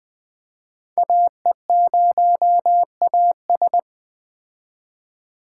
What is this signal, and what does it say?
Morse code, heard as a keyed tone, about 21 wpm, 700 Hz. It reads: AE0AS
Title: Speed Racing - USA 2x2 Call Signs 17wpm